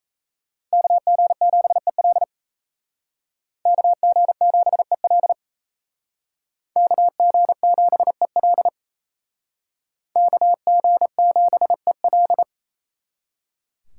Morse code, heard as a keyed tone, about 35 wpm, 700 Hz. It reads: KG7EL KG7EL KG7EL KG7EL
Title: Speed Racing - USA Call Signs 28wpm